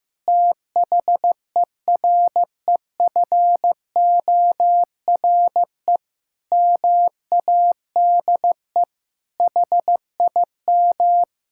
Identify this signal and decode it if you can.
Morse code, heard as a keyed tone, about 16 wpm, 700 Hz. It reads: THEREFORE MADE HIM
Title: Sentences from Top 1000 Words - Review Part 5 - 15wpm